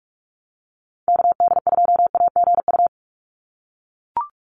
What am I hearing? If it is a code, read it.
Morse code, heard as a keyed tone, about 44 wpm, 700 Hz. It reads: KD2AGU
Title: USA Call Signs 45wpm